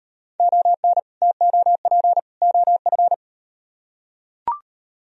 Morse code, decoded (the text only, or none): ON TOP OF